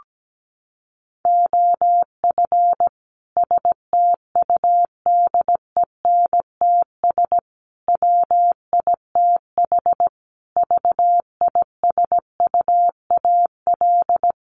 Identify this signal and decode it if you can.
Morse code, heard as a keyed tone, about 17 wpm, 700 Hz. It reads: OF STUDENTS WITH VISUAL
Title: Sets of 5 Words Encore 17wpm